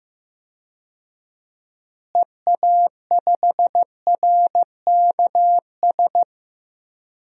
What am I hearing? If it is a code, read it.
Morse code, heard as a keyed tone, about 15 wpm, 700 Hz. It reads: EA5RKS
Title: DXCC-100 Call Signs - 6 Characters - Repeated 2x - 15wpm